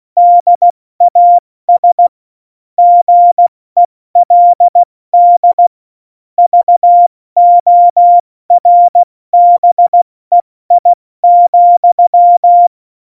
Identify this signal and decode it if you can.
Morse code, heard as a keyed tone, about 16 wpm, 700 Hz. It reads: DAS GELD VORBEI,
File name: Morsekurs / Witze / Witz 24. November 2020, 70 F80